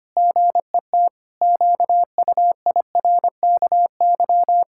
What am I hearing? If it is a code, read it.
Morse code, heard as a keyed tone, about 25 wpm, 700 Hz. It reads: GET QUIRKY
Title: Pangram Challenge v2 - 25wpm